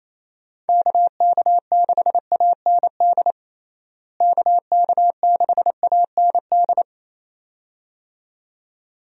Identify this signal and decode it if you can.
Morse code, heard as a keyed tone, about 28 wpm, 700 Hz. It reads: KK6AND KK6AND
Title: USA Call Signs - 6 Characters - Repeated 2x - 28wpm